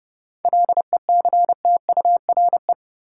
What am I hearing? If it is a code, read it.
Morse code, heard as a keyed tone, about 30 wpm, 700 Hz. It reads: LECTURE
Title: Sentences from Top 1600 Words - New Word Drill - 30wpm